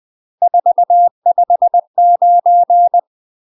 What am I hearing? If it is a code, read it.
Morse code, heard as a keyed tone, about 20 wpm, 700 Hz. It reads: 459
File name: UK MASTER QSOs TRAINING / 20WPM / UK MASTER QSOs 0046